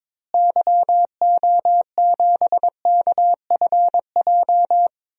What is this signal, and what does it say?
Morse code, heard as a keyed tone, about 22 wpm, 700 Hz. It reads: YO7KFJ
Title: DXCC-100 Call Signs - 6 Characters - Repeated 2x - 22wpm